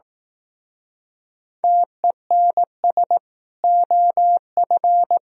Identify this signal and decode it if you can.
Morse code, heard as a keyed tone, about 18 wpm, 700 Hz. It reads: TENS OF
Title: Speed-Racing - Sets of 4 Words 15wpm